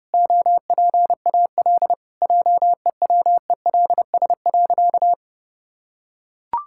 Morse code, heard as a keyed tone, about 30 wpm, 700 Hz. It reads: OPAL JEWELS.
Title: Pangram Challenge v2 - 30wpm